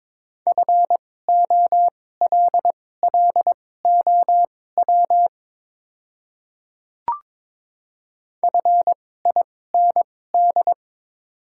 Morse code, heard as a keyed tone, about 22 wpm, spaced as for 16 wpm, 700 Hz. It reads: FOLLOW FIND
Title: 2x Character Spacing - Top 200 Words - 22wpm